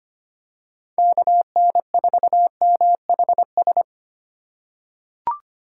Morse code, heard as a keyed tone, about 25 wpm, 700 Hz. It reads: KN4MHS
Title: USA Call Signs - 6 Characters - 2x Character Spacing - 25wpm